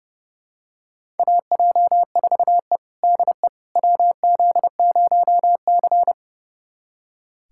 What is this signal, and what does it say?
Morse code, heard as a keyed tone, about 32 wpm, 700 Hz. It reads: AJ4E DE WZ0C
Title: USA Callsign DE Callsign - 4 Characters - 30wpm